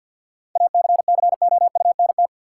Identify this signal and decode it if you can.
Morse code, heard as a keyed tone, about 53 wpm, 700 Hz. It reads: ACCOUNT
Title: Sentences from Top 900 Words - New Word Drill - 50wpm